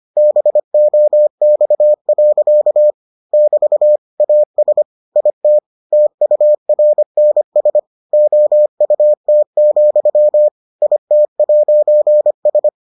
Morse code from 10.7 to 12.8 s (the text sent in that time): IT'S